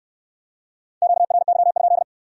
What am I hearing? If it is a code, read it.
Morse code, heard as a keyed tone, about 65 wpm, 700 Hz. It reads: QRY?